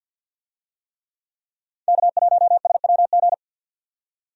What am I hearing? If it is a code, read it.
Morse code, heard as a keyed tone, about 47 wpm, 700 Hz. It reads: K1SWG
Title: USA Call Signs - 5 Characters - 50wpm